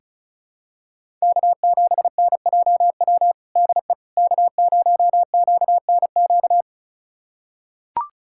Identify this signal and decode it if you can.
Morse code, heard as a keyed tone, about 36 wpm, 700 Hz. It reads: K7NJW DE K0QNQ
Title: USA Callsign DE Callsign - 5 Characters - 35wpm